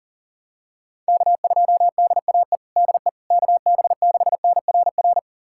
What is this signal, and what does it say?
Morse code, heard as a keyed tone, about 40 wpm, 700 Hz. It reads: K2DAE DE KB6NRR
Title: USA Callsign DE Callsign 40wpm